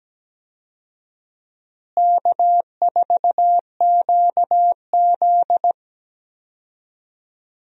Morse code, heard as a keyed tone, about 17 wpm, 700 Hz. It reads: K4QZ
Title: USA Call Signs - 4 Characters - Repeated 2x - 17wpm